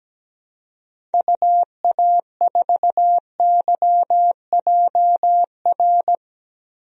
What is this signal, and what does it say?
Morse code, heard as a keyed tone, about 17 wpm, 700 Hz. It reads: UA4YJR
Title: DXCC-100 Call Signs - 6 Characters - Repeated 3x - 17wpm